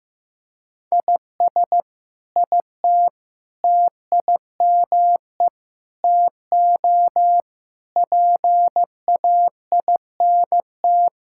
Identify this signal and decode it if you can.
Morse code, heard as a keyed tone, about 15 wpm, 700 Hz. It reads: IS IT TIME TO PAINT
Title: Sentences from Top 1000 Words - Review Part 6 - 15wpm